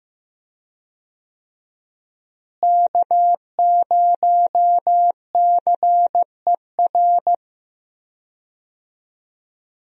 Morse code, heard as a keyed tone, about 15 wpm, 700 Hz. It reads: K0CER
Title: USA Call Signs - 5 Characters - Repeated 2x - 15wpm